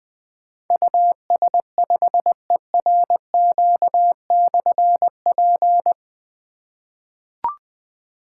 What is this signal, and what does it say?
Morse code, heard as a keyed tone, about 20 wpm, 700 Hz. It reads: US5ERQ/P
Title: Difficult Call Signs 20wpm